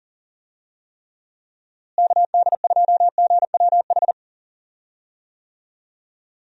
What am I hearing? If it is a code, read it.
Morse code, heard as a keyed tone, about 40 wpm, 700 Hz. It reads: KD2GWH
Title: USA Call Signs - 6 Characters - Repeated 3x - 40wpm